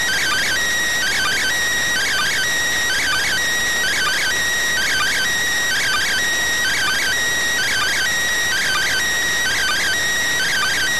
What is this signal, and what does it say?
Two-tone teleprinter signal, about 400 Hz shift.